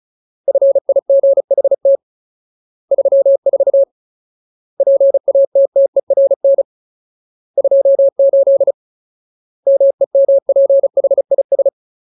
Morse code, heard as a keyed tone, about 34 wpm, 550 Hz. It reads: FIGHT 34 PATTERN 28 MEMPHIS
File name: Ultimate Word List / 35WPM Ultimate